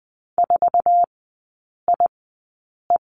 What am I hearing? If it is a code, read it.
Morse code, heard as a keyed tone, about 20 wpm, spaced as for 8 wpm, 700 Hz. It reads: 4IE
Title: 047 Lesson 9e - Number 4 - Three Characters 20wpm